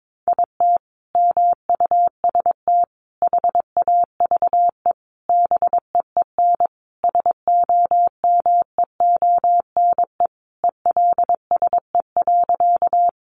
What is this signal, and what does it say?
Morse code, heard as a keyed tone, about 22 wpm, 700 Hz. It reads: IT MUST HAVE BEEN SOMEONE ELSE.
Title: Sentences from Top 800 Words 22wpm